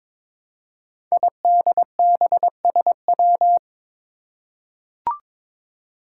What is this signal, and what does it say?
Morse code, heard as a keyed tone, about 22 wpm, 700 Hz. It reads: IDBSW
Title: Five Challenging Letter-Number Combinations 22wpm